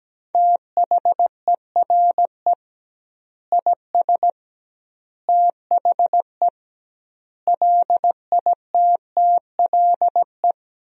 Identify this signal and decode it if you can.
Morse code, heard as a keyed tone, about 17 wpm, 700 Hz. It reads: THERE IS THE LITTLE
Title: 2x Word-Spacing - Sentences from Top 200 Words - 17wpm